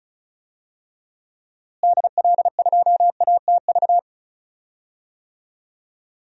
Morse code, heard as a keyed tone, about 36 wpm, 700 Hz. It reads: DL2ATV
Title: DXCC-100 Call Signs - 6 Characters - Repeated 3x - 35wpm